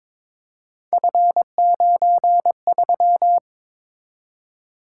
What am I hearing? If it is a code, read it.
Morse code, heard as a keyed tone, about 22 wpm, 700 Hz. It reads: F93
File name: practice / 22wpm / Three Letter-Number Combinations - 22wpm